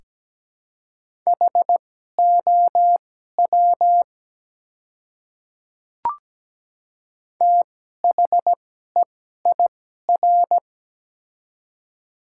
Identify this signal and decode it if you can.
Morse code, heard as a keyed tone, about 17 wpm, spaced as for 12 wpm, 700 Hz. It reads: HOW THEIR
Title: 2x Character Spacing - Top 100 Words - 17wpm